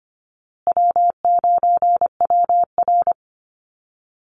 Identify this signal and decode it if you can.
Morse code, heard as a keyed tone, about 25 wpm, 700 Hz. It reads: W9WR
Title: Easy Call Signs Repeated 3x - 25wpm